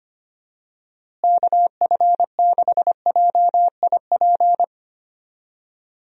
Morse code, heard as a keyed tone, about 25 wpm, 700 Hz. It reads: KF6JIP
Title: USA Call Signs - 6 Characters - Repeated 3x - 25wpm